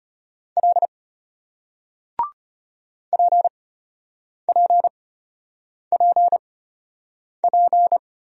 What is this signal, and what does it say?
Morse code, heard as a keyed tone, about 33 wpm, spaced as for 8 wpm, 700 Hz. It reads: L PPPP